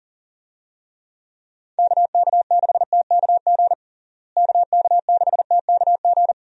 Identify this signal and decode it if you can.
Morse code, heard as a keyed tone, about 40 wpm, 700 Hz. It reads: KK6TKG KK6TKG
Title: USA Call Signs - 6 Characters - Repeated 3x - 40wpm